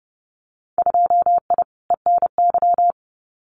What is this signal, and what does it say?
Morse code, heard as a keyed tone, about 30 wpm, 700 Hz. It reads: 2I ENY
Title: Winter Field Day Exchanges Encore 30wpm